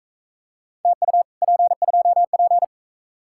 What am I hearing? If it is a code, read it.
Morse code, heard as a keyed tone, about 44 wpm, 700 Hz. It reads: TU P2P